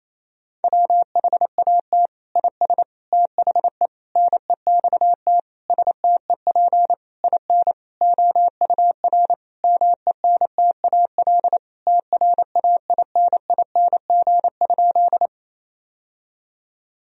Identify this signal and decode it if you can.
Morse code, heard as a keyed tone, about 28 wpm, 700 Hz. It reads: WHAT IS THE NEXT STEP IN OUR MENTAL TRAINING?
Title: Sentences from Top 1000 Words - Review Part 4 - 28wpm